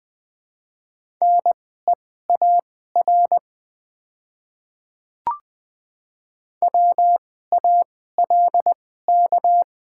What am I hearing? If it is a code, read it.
Morse code, heard as a keyed tone, about 20 wpm, spaced as for 14 wpm, 700 Hz. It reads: NEAR WALK
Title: 2x Character Spacing - Top 300 Words - 20wpm